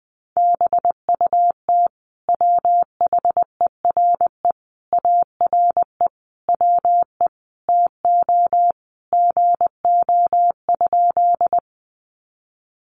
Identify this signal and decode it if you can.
Morse code, heard as a keyed tone, about 20 wpm, 700 Hz. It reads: BUT WHERE ARE WE TO GO?
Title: Sentences from Top 200 Words 20wpm